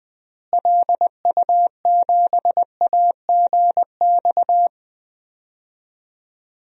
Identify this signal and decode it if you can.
Morse code, heard as a keyed tone, about 20 wpm, 700 Hz. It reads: LU7AGX
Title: DXCC-100 Call Signs - 6 Characters - Repeated 3x - 20wpm